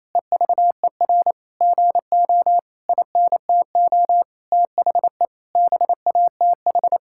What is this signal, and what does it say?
Morse code, heard as a keyed tone, about 28 wpm, 700 Hz. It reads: EVER GO INTO THE BATH
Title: Sentences from Top 1600 Words 28wpm